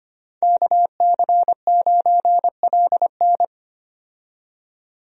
Morse code, heard as a keyed tone, about 25 wpm, 700 Hz. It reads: KC9LN
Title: USA Call Signs - 5 Characters - 25wpm